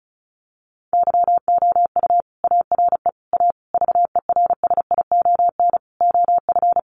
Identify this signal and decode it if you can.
Morse code, heard as a keyed tone, about 34 wpm, 700 Hz. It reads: YOU ARE A VERSION OF
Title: Sentences from Top 1400 Words 35wpm